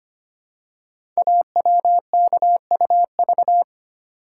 Morse code, heard as a keyed tone, about 25 wpm, 700 Hz. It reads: AWKUV
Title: Code Groups - 5 Letters - 25wpm